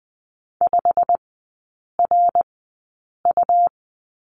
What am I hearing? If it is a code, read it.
Morse code, heard as a keyed tone, about 20 wpm, spaced as for 8 wpm, 700 Hz. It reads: 5RU
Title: 100 Lesson 17e - Letter u - Three Characters 20wpm